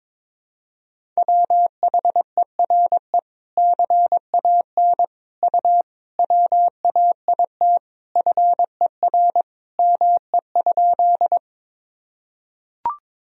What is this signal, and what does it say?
Morse code, heard as a keyed tone, about 22 wpm, 700 Hz. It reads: WHERE CAN U WAIT FER ME?
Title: Sentences from Top 500 Words and Abbreviations 22wpm